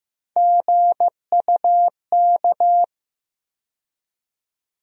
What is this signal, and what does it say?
Morse code, heard as a keyed tone, about 15 wpm, 700 Hz. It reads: GUK
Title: USA Call Sign Postfixes - 3 Letters 15wpm